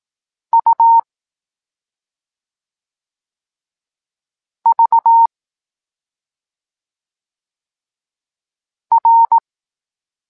Morse code, heard as a keyed tone, about 18 wpm, 920 Hz. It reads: U V R
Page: Tekst 23.3 – Ciąg dalszy doskonalenia odbioru poznanych liter